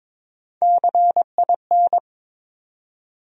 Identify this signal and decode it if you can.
Morse code, heard as a keyed tone, about 21 wpm, 700 Hz. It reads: CIN
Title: USA Call Sign Postfixes - 3 Letters 22wpm